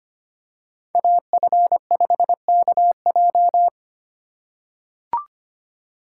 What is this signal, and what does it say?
Morse code, heard as a keyed tone, about 25 wpm, 700 Hz. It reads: AF5KJ
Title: USA Call Signs - 3x Character Spacing - 25wpm